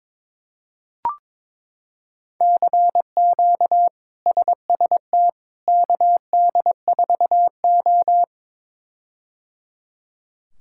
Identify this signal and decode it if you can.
Morse code, heard as a keyed tone, about 22 wpm, 700 Hz. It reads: CQ SST KD4O
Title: K1USN SST QSO Practice - 22wpm